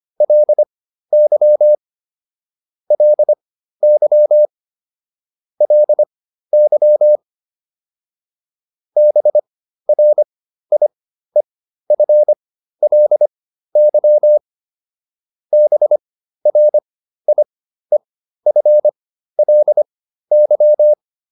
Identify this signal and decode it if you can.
Morse code, heard as a keyed tone, about 25 wpm, spaced as for 13 wpm, 600 Hz. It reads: LY LY LY BRIEFLY BRIEFLY